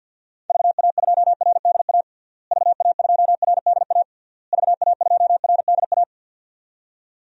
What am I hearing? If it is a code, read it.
Morse code, heard as a keyed tone, about 46 wpm, 700 Hz. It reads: VA2RDA VA2RDA VA2RDA
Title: DXCC-100 Call Signs - 6 Characters - Repeated 3x - 50wpm